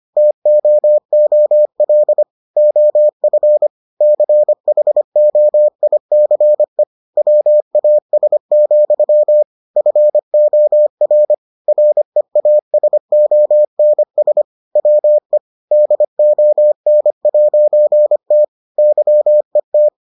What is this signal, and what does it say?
Morse code, heard as a keyed tone, about 25 wpm, 600 Hz. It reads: TOOL OF CHOICE WAS, FOR REASONS WE DON'T YET